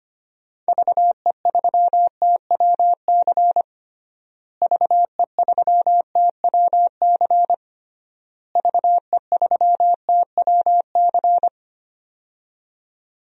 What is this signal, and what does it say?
Morse code, heard as a keyed tone, about 25 wpm, 700 Hz. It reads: VE3TWC VE3TWC VE3TWC
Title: DXCC-100 Call Signs - 6 Characters - Repeated 3x - 25wpm